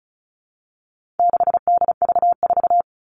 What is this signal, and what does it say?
Morse code, heard as a keyed tone, about 34 wpm, 700 Hz. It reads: 6DV4
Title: All Cut Numbers - 4-Digit 35wpm